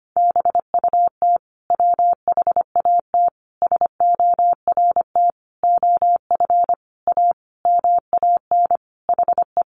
Morse code, heard as a keyed tone, about 25 wpm, 700 Hz. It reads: BUT WHAT SORT OF A MAN HE
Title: Sentences from Top 1300 Words 25wpm